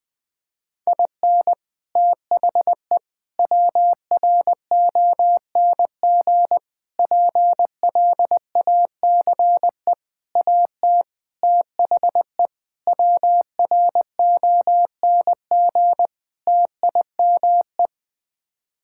Morse code, heard as a keyed tone, about 21 wpm, 700 Hz. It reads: IN THE WRONG PLACE AT THE WRONG TIME
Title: Sets of 8 Words - 20wpm